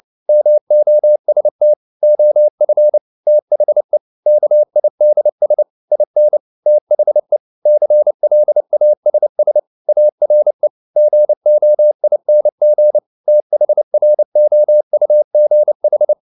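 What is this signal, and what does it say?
Morse code, heard as a keyed tone, about 29 wpm, 600 Hz. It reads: MOST OF THE KIDS IN THE CLASS ARE GOING THROUGH
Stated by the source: Morse code at 29 wpm, sending Ken Follett's novel Never in English, from CW med Gnister.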